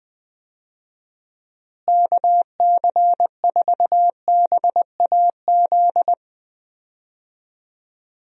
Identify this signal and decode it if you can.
Morse code, heard as a keyed tone, about 20 wpm, 700 Hz. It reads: KC4BAZ
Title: USA Call Signs - 6 Characters - Repeated 3x - 20wpm